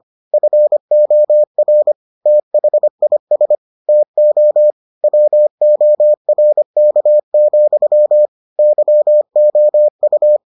Morse code, heard as a keyed tone, about 25 wpm, 600 Hz. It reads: FOR THIS TO WORK, YOU